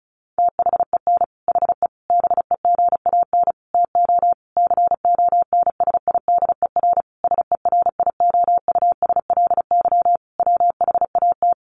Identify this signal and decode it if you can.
Morse code, heard as a keyed tone, about 34 wpm, 700 Hz. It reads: THEN HE BEGAN TO CONSIDER SERIOUSLY WHAT
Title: Sentences from Top 1200 Words 35wpm